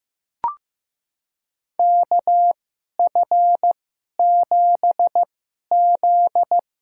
Morse code, heard as a keyed tone, about 16 wpm, spaced as for 11 wpm, 700 Hz. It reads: KF7Z
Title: USA Call Signs - 6 Characters - 2x Character Spacing - 15wpm